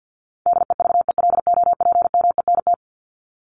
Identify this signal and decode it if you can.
Morse code, heard as a keyed tone, about 50 wpm, 700 Hz. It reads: DEVELOPMENT
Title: Top 501-750 Words 50wpm